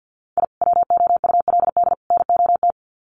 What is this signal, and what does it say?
Morse code, heard as a keyed tone, about 50 wpm, 700 Hz. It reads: I WOULD NOT
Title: Sets of 3 Words 50wpm